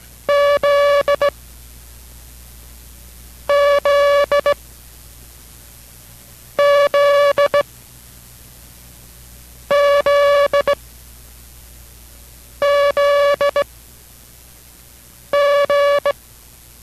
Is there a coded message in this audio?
Yes: Morse code from 0.3 to 16.1 s, heard as a keyed tone, about 18 wpm, spaced as for 4 wpm, 570 Hz. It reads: ZZZZZG